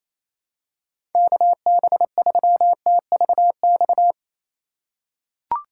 Morse code, heard as a keyed tone, about 28 wpm, 700 Hz. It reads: KB3TVX
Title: USA Call Signs - 6 Characters - 3x Character Spacing - 28wpm